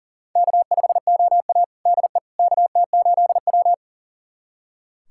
Morse code, heard as a keyed tone, about 40 wpm, 700 Hz. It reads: K5OA DE KT8W
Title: USA Callsign DE Callsign - 4 Characters - 40wpm